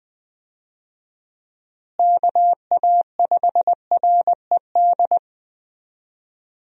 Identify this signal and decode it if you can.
Morse code, heard as a keyed tone, about 20 wpm, 700 Hz. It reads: KA5RED
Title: USA Call Signs - 6 Characters - Repeated 3x - 20wpm